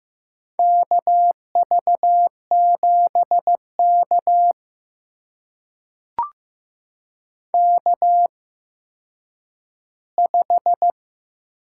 Morse code, heard as a keyed tone, about 15 wpm, spaced as for 9 wpm, 700 Hz. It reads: KV7K K5
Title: USA Call Signs - 4 Characters - 8x Character Spacing - 15wpm